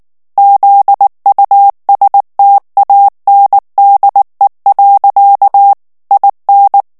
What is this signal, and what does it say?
Morse code, heard as a keyed tone, about 19 wpm, 800 Hz. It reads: ZUSTANDE. IN
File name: morse mp3 / ALLEMAND texte clair 2 vitesse 18 mots minute